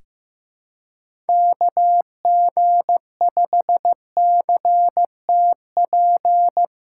Morse code, heard as a keyed tone, about 16 wpm, 700 Hz. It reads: KG5CTP
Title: USA Callsign DE Callsign - 6 Characters - 15wpm